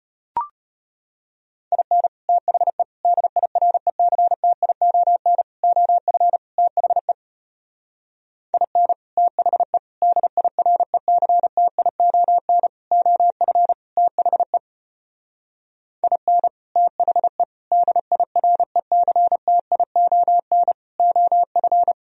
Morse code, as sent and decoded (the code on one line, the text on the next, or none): .. -. / - .... / -.. .. .-. -.-. - .. --- -. / --- ..-. / - .... / .. -. / - .... / -.. .. .-. -.-. - .. --- -. / --- ..-. / - .... / .. -. / - .... / -.. .. .-. -.-. - .. --- -. / --- ..-.
IN THE DIRECTION OF THE IN THE DIRECTION OF THE IN THE DIRECTION OF